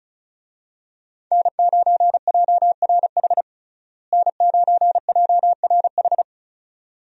Morse code, heard as a keyed tone, about 36 wpm, 700 Hz. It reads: N9JRH N9JRH
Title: USA Call Signs - 5 Characters - Repeated 2x - 35wpm